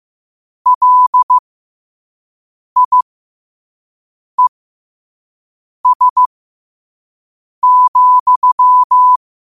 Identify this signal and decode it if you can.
Morse code, heard as a keyed tone, about 15 wpm, spaced as for 5 wpm, 1000 Hz. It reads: LIES,
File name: qotdfiles / 5wpm